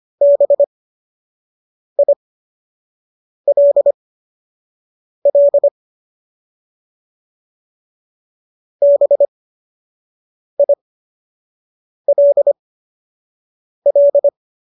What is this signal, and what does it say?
Morse code, heard as a keyed tone, about 25 wpm, spaced as for 6 wpm, 570 Hz. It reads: BILL BILL